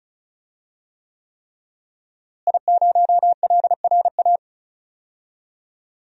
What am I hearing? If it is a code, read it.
Morse code, heard as a keyed tone, about 34 wpm, 700 Hz. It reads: I0LRA
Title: DXCC-100 Call Signs - 5 Characters - Repeated 3x - 35wpm